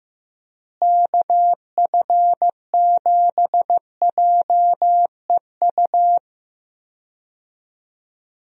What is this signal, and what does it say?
Morse code, heard as a keyed tone, about 16 wpm, 700 Hz. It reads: KF7JEU
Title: USA Call Signs - 6 Characters - Repeated 3x - 15wpm